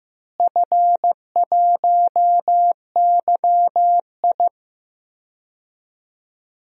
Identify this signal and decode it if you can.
Morse code, heard as a keyed tone, about 15 wpm, 700 Hz. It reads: F1YI
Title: DXCC-100 Call Signs - 4 Characters - Repeated 3x - 15wpm